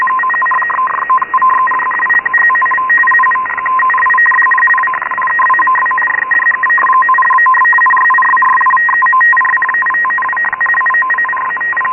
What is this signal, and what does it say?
Two-tone teleprinter signal, about 850 Hz shift.
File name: Radio / SDR Audio / gqrx